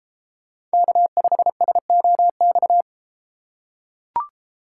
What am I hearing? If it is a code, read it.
Morse code, heard as a keyed tone, about 33 wpm, 700 Hz. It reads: K5SOX